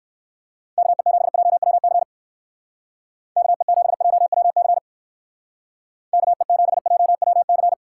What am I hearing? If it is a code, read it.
Morse code, heard as a keyed tone, about 54 wpm, 700 Hz. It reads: KE7JWC KE7JWC KE7JWC
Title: Speed Racing - USA 2x3 Call Signs 45wpm